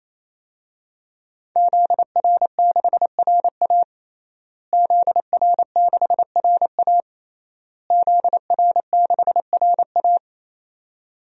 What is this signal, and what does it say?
Morse code, heard as a keyed tone, about 28 wpm, 700 Hz. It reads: ZR6RA ZR6RA ZR6RA
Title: DXCC-100 Call Signs - 5 Characters - Repeated 3x - 28wpm